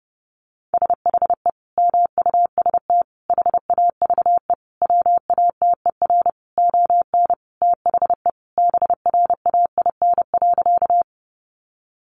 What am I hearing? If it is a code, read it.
Morse code, heard as a keyed tone, about 30 wpm, 700 Hz. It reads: SHE MUST HAVE WATER ON THE BRAIN.
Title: Sentences from Top 1100 Words 30wpm